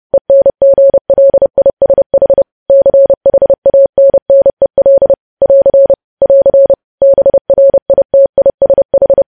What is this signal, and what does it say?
Morse code, heard as a keyed tone, about 30 wpm, 550 Hz. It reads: ENGLISH CHANNEL <AR> <AR> BRITISH
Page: News Headlines in Morse Code at 30 WPM